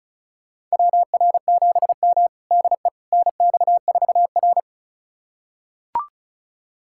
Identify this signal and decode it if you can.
Morse code, heard as a keyed tone, about 36 wpm, 700 Hz. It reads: WR7M DE NX4R
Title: USA Callsign DE Callsign - 4 Characters - 35wpm